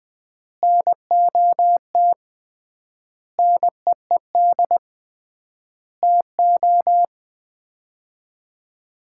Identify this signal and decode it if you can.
Morse code, heard as a keyed tone, about 20 wpm, 700 Hz. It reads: NOT NEED TO
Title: 3x Word-Spacing - Sets of 5 Words - 20wpm